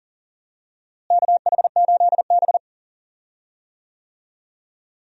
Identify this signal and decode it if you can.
Morse code, heard as a keyed tone, about 40 wpm, 700 Hz. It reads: KH8B
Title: USA Call Signs - 4 Characters - Repeated 2x - 40wpm